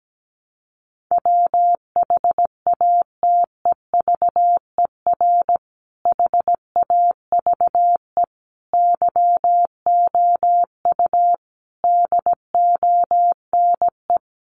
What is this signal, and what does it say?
Morse code, heard as a keyed tone, about 17 wpm, 700 Hz. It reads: WHATEVER HAVE YOU DONE
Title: Sentences from Top 1100 Words 17wpm